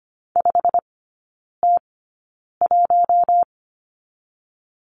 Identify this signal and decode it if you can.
Morse code, heard as a keyed tone, about 25 wpm, spaced as for 9 wpm, 690 Hz. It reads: 5T1
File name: Lesson 15e - Number 5 - Three Characters 25wpm